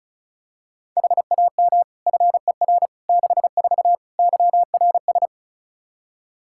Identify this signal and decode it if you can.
Morse code, heard as a keyed tone, about 36 wpm, 700 Hz. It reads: HAM FER 64 YRS
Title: Practice QSO - Exchange 3 - More Info - 35wpm